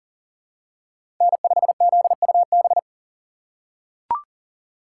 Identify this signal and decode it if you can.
Morse code, heard as a keyed tone, about 40 wpm, 700 Hz. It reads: N5ZUB
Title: USA Call Signs - 5 Characters - 2x Character Spacing - 40wpm